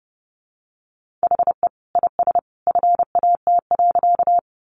Morse code, heard as a keyed tone, about 30 wpm, 700 Hz. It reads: HE IS FAT.
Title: Easy Phrases 30wpm